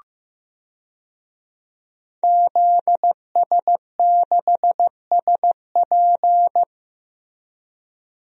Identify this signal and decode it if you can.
Morse code, heard as a keyed tone, about 15 wpm, 700 Hz. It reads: ZS6SP
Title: DXCC-100 Call Signs - 5 Characters - Repeated 3x - 15wpm